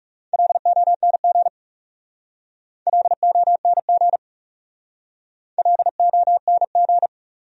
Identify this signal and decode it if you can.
Morse code, heard as a keyed tone, about 36 wpm, 700 Hz. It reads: LONG LONG LONG
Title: Speed-Racing Top 200 Words 30wpm